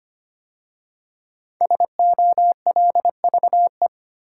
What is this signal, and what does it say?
Morse code, heard as a keyed tone, about 25 wpm, 700 Hz. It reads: SOLVE
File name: 5 Letter Words - First 3 Letters Given - 25wpm